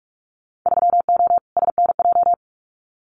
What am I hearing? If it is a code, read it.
Morse code, heard as a keyed tone, about 44 wpm, 700 Hz. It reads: 3O SNJ
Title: Winter Field Day Exchanges Encore 45wpm